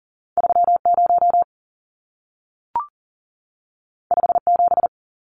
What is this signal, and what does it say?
Morse code, heard as a keyed tone, about 40 wpm, 700 Hz. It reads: 30 57
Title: Numbers - 2-Digit 40wpm